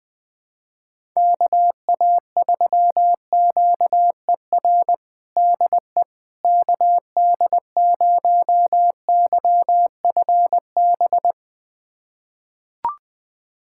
Morse code, heard as a keyed tone, about 20 wpm, 700 Hz. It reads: KA3QER DE KD0YFB
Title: USA Callsign DE Callsign - 6 Characters - 20wpm